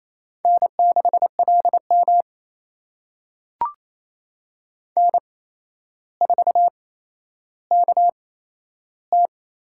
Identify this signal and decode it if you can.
Morse code, heard as a keyed tone, about 28 wpm, spaced as for 12 wpm, 700 Hz. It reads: N6LM N4KT
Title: USA Call Signs - 4 Characters - 8x Character Spacing - 28wpm